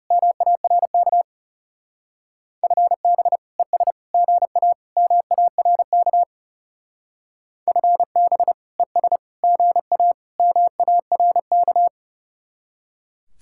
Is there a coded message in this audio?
Yes: Morse code from 0.1 to 11.9 s, heard as a keyed tone, about 38 wpm, 700 Hz. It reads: MARK FB ES GA MARK FB ES GA MARK